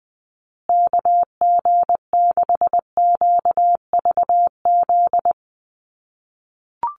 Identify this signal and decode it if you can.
Morse code, heard as a keyed tone, about 20 wpm, 700 Hz. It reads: KG6QVZ
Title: USA Call Signs Repeated 3x 20wpm